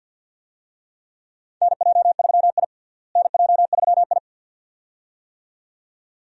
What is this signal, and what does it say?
Morse code, heard as a keyed tone, about 52 wpm, 700 Hz. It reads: NJ3I NJ3I
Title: USA Call Signs - 4 Characters - Repeated 2x - 50wpm